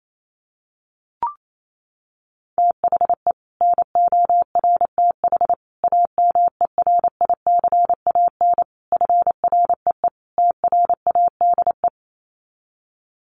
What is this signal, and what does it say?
Morse code, heard as a keyed tone, about 28 wpm, 700 Hz. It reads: THE NORTH AMERICAN FREE TRADE
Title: Sets of 5 Words Encore 28wpm